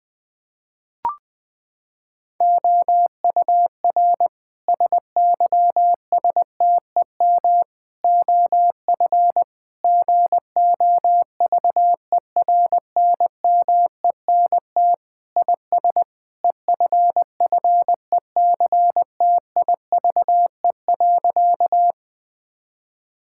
Morse code, heard as a keyed tone, about 20 wpm, 700 Hz. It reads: OUR SYSTEM OF GOVERNMENT IS EFFECTIVE.
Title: Sentences from Top 1000 Words - Review Part 6 - 20wpm